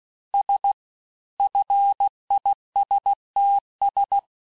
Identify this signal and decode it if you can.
Morse code, heard as a keyed tone, about 16 wpm, 800 Hz. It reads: S FISTS
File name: cours-du-lundi files / cours-1592--teste-FIN-de-QSO-vitesse-15mpm